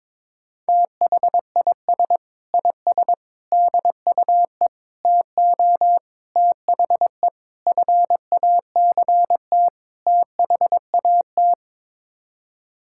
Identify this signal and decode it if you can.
Morse code, heard as a keyed tone, about 22 wpm, 700 Hz. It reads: THIS IS DUE TO THE FACT THAT
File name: practice / 22wpm / Sets of 7 Words - 22wpm